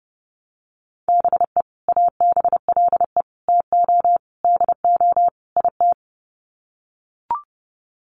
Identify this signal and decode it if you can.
Morse code, heard as a keyed tone, about 30 wpm, 700 Hz. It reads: BE ABLE TO DO IT
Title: Sets of 5 Words Encore2 30wpm